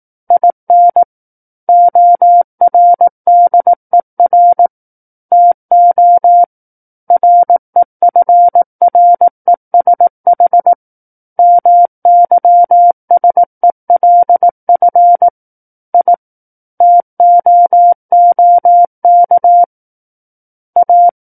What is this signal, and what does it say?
Morse code, heard as a keyed tone, about 18 wpm, 700 Hz. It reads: IN ORDER TO REFRESH MYSELF I TOOK A
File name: 18 WPM